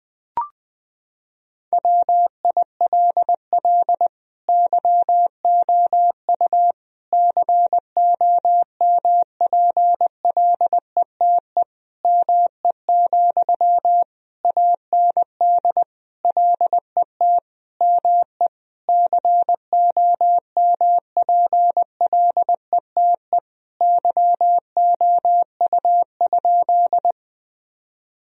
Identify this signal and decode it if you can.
Morse code, heard as a keyed tone, about 20 wpm, 700 Hz. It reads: WILL YOU COMPLETE ME, AND LET ME COMPLETE YOU?
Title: Sentences from Top 1000 Words - Review Part 2 - 20wpm